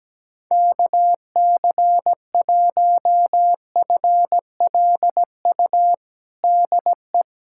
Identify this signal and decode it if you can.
Morse code, heard as a keyed tone, about 17 wpm, 700 Hz. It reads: KC1FLU DE
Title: USA Callsign DE Callsign - 6 Characters - 17wpm